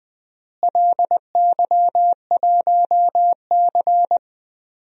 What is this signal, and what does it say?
Morse code, heard as a keyed tone, about 20 wpm, 700 Hz. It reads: LY1C
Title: DXCC-100 Call Signs - 4 Characters - Repeated 3x - 20wpm